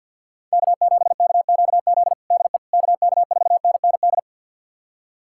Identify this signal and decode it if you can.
Morse code, heard as a keyed tone, about 47 wpm, 690 Hz. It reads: K7KQZ DE KK4NND
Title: USA Callsign DE Callsign 50wpm